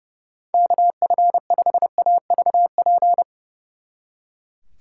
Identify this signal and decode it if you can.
Morse code, heard as a keyed tone, about 37 wpm, 700 Hz. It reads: KF5AVP
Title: Speed Racing - USA 2x3 Call Signs 30wpm